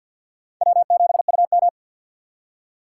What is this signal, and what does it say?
Morse code, heard as a keyed tone, about 47 wpm, 700 Hz. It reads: W7UM
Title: USA Call Signs - 4x Character Spacing - 50wpm